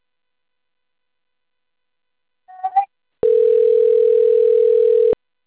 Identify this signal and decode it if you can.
Ringback tone (North America), 440 + 480 Hz.
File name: out- - - - - .20600